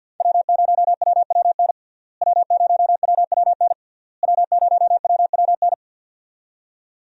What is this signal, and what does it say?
Morse code, heard as a keyed tone, about 47 wpm, 700 Hz. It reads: W0WWN W0WWN W0WWN
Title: USA Call Signs - 5 Characters - Repeated 3x - 50wpm